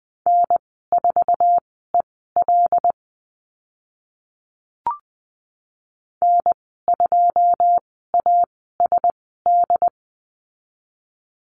Morse code, heard as a keyed tone, about 20 wpm, spaced as for 15 wpm, 700 Hz. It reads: N4EL N2ASD
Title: 082 Lesson 14g - Number 2 - Call Signs 20wpm